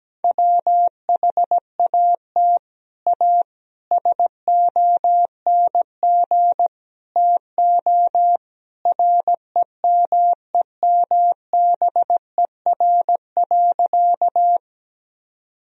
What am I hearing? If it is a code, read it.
Morse code, heard as a keyed tone, about 17 wpm, 700 Hz. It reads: WHAT A SONG TO REMEMBER.
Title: Sentences from Top 1000 Words - Review Part 2 - 17wpm